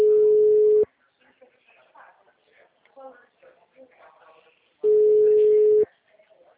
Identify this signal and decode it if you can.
Ringback tone (Europe, 425 Hz), with speech.
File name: out- -2004- - - .25075